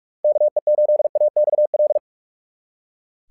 Morse code, heard as a keyed tone, about 44 wpm, 600 Hz. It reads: KE8AXL
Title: SOTA QSO Practice 45wpm